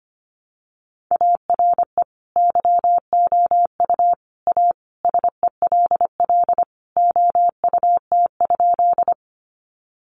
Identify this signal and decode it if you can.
Morse code, heard as a keyed tone, about 25 wpm, 700 Hz. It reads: ARE YOU A SELL OUT?
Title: Sentences from Top 1100 Words 25wpm